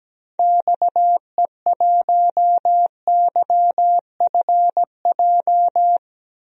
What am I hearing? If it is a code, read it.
Morse code, heard as a keyed tone, about 17 wpm, 700 Hz. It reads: XE1YFJ
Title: DXCC-100 Call Signs - 6 Characters - 17wpm